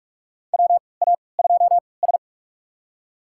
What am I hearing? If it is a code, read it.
Morse code, heard as a keyed tone, about 46 wpm, spaced as for 26 wpm, 700 Hz. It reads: WA2S
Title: USA Call Signs - 4 Characters - 3x Character Spacing - 45wpm